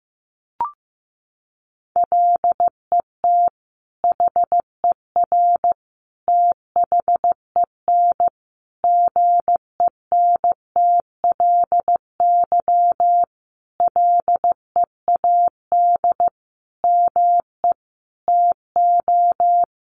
Morse code, heard as a keyed tone, about 15 wpm, 700 Hz. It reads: LET HER THEN GENTLY LEAD ME TO
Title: Sentences from Top 1500 Words 15wpm